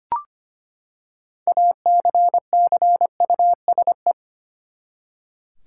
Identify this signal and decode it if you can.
Morse code, heard as a keyed tone, about 25 wpm, 700 Hz. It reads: ACCUSE
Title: Six Letter Words - 25wpm